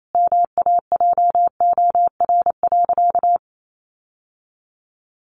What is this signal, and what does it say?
Morse code, heard as a keyed tone, about 28 wpm, 700 Hz. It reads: MAJOR.
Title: Sentences from Top 600 Words 28wpm